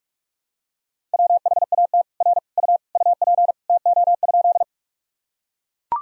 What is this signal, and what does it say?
Morse code, heard as a keyed tone, about 41 wpm, 700 Hz. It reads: WHAT R U UP TO?
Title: Sentences from Top 100 Words and Abbreviations 45wpm